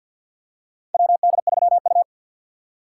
Morse code, heard as a keyed tone, about 53 wpm, 700 Hz. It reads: WD3U
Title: USA Call Signs - 4 Characters - 2x Character Spacing - 50wpm